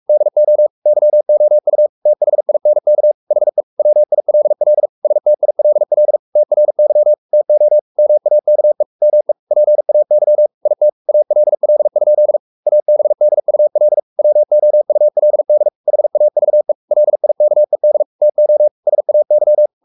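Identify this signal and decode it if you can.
Morse code, heard as a keyed tone, about 44 wpm, 600 Hz. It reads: DO YOU THINK HE WILL STILL TRY TO MAKE ME PAY IT ALL? ABDUL WOULD HAVE LIKED TO SAY
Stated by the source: Morse code at 43 wpm sending Ken Follett's novel Never in English, from CW med Gnister.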